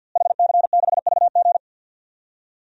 Morse code, heard as a keyed tone, about 46 wpm, 700 Hz. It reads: HC6VG
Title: DXCC-100 Call Signs - 5 Characters - Repeated 3x - 50wpm